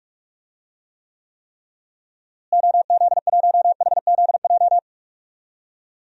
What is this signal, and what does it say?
Morse code, heard as a keyed tone, about 41 wpm, 700 Hz. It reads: OZ1HZJ
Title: DXCC-100 Call Signs - 6 Characters - Repeated 3x - 45wpm